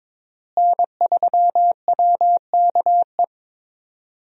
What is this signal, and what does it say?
Morse code, heard as a keyed tone, about 22 wpm, 700 Hz. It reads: N3WKE
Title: USA Call Signs - 5 Characters - Repeated 2x - 22wpm